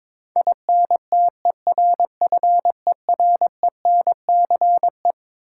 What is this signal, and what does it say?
Morse code, heard as a keyed tone, about 22 wpm, 700 Hz. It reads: INTERFERENCE
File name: Twelve Letter Words - 22wpm